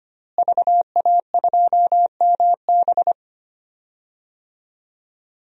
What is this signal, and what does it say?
Morse code, heard as a keyed tone, about 25 wpm, 700 Hz. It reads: VA2MB
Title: DXCC-100 Call Signs - 5 Characters - Repeated 3x - 25wpm